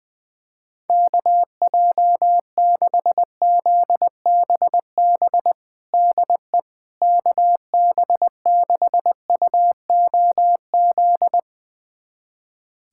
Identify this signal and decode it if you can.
Morse code, heard as a keyed tone, about 21 wpm, 700 Hz. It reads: KJ6ZBB DE KB6UOZ
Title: USA Callsign DE Callsign 20wpm